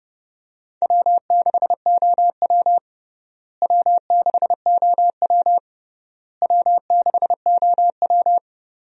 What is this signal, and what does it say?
Morse code, heard as a keyed tone, about 30 wpm, 700 Hz. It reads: W6OW W6OW W6OW
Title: USA Call Signs - 4 Characters - Repeated 3x - 30wpm